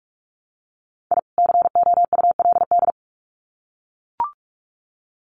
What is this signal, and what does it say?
Morse code, heard as a keyed tone, about 44 wpm, 700 Hz. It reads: I COULD
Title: Sets of 2 Words 45wpm